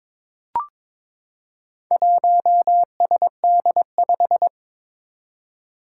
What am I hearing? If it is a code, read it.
Morse code, heard as a keyed tone, about 23 wpm, 700 Hz. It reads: 1SD5
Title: Four Challenging Letter-Number Combinations 22wpm